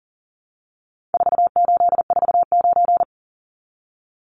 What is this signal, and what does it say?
Morse code, heard as a keyed tone, about 40 wpm, 700 Hz. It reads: 4849
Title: Numbers - 4-Digit 40wpm